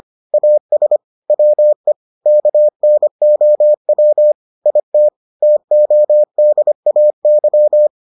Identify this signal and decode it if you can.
Morse code, heard as a keyed tone, about 25 wpm, 600 Hz. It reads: AS WE KNOW IT TODAY